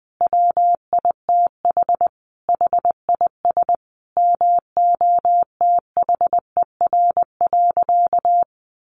Morse code, heard as a keyed tone, about 20 wpm, 700 Hz. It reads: WITH HIS MOTHER.
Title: Sentences from Top 1300 Words 20wpm